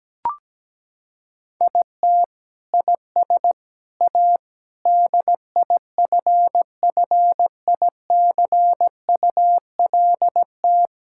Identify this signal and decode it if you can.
Morse code, heard as a keyed tone, about 17 wpm, 700 Hz. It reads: IT IS A DIFFICULT
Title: Sentences from Top 1000 Words - Review Part 6 - 17wpm